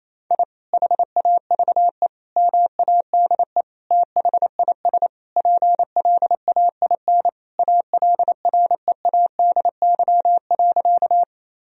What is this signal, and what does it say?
Morse code, heard as a keyed tone, about 28 wpm, 700 Hz. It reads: I HAVE MADE THIS PLAIN ALREADY.
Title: Sentences from Top 1000 Words - Review Part 2 - 28wpm